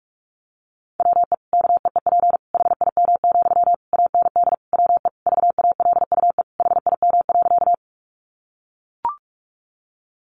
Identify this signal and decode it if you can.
Morse code, heard as a keyed tone, about 44 wpm, 700 Hz. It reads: WE KEEP HIM, AND WE VALUE HIM.
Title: Sentences from Top 600 Words 45wpm